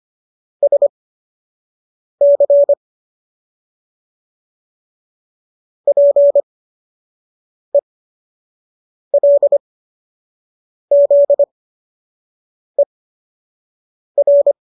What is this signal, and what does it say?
Morse code, heard as a keyed tone, about 25 wpm, spaced as for 6 wpm, 570 Hz. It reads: SC PELZER